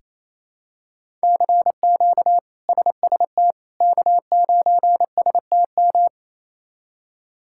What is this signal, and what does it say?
Morse code, heard as a keyed tone, about 28 wpm, 700 Hz. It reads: CQ SST K9STM
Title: K1USN SST QSO Practice - 28wpm